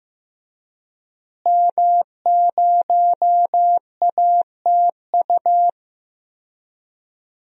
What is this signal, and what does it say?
Morse code, heard as a keyed tone, about 15 wpm, 700 Hz. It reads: M0ATU
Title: DXCC-100 Call Signs - 5 Characters - Repeated 3x - 15wpm